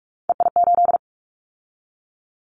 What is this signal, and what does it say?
Morse code, heard as a keyed tone, about 46 wpm, 700 Hz. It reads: EI8
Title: Callsign Prefixes for Top 50 DXCC - 45wpm